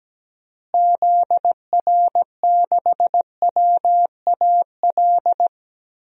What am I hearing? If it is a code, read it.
Morse code, heard as a keyed tone, about 17 wpm, 700 Hz. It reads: ZR6WAL
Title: DXCC-100 Call Signs - 6 Characters - Repeated 3x - 17wpm